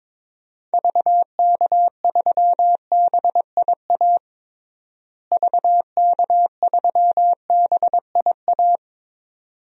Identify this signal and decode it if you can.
Morse code, heard as a keyed tone, about 22 wpm, 700 Hz. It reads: VK3BIA VK3BIA
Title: DXCC-100 Call Signs - 6 Characters - Repeated 2x - 22wpm